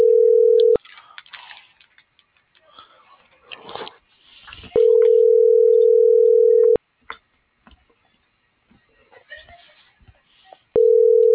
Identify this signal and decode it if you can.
Ringback tone (North America), 440 + 480 Hz.